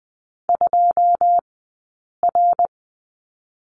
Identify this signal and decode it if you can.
Morse code, heard as a keyed tone, about 20 wpm, spaced as for 8 wpm, 700 Hz. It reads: 2R
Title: 080 Lesson 14d - Number 2 - Two Characters 20wpm